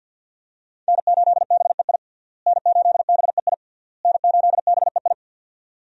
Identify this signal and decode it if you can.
Morse code, heard as a keyed tone, about 50 wpm, 700 Hz. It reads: N8BEI N8BEI N8BEI
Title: USA Call Signs - 5 Characters - Repeated 3x - 50wpm